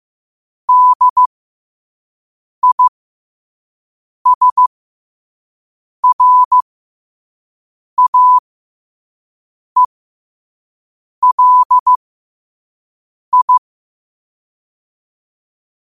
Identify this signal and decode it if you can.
Morse code, heard as a keyed tone, about 15 wpm, spaced as for 5 wpm, 1000 Hz. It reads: DISRAELI
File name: qotdfiles / 5wpm